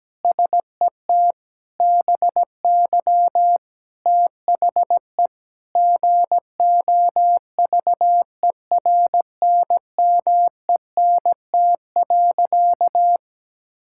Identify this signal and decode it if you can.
Morse code, heard as a keyed tone, about 17 wpm, 700 Hz. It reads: SET BY THE GOVERNMENT.
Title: Sentences from Top 1000 Words - Review Part 6 - 17wpm